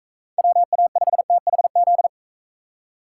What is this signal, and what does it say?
Morse code, heard as a keyed tone, about 42 wpm, 700 Hz. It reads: WA5THZ